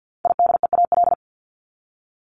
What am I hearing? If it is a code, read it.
Morse code, heard as a keyed tone, about 50 wpm, 700 Hz. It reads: IDEAL
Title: Top 1251-1500 Words 50wpm